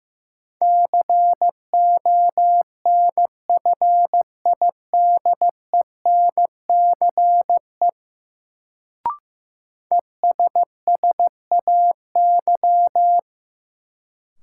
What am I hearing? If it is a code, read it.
Morse code, heard as a keyed tone, about 16 wpm, 700 Hz. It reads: CONFIDENCE ESSAY
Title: Sentences from Top 1500 Words - New Word Drill Alt - 15wpm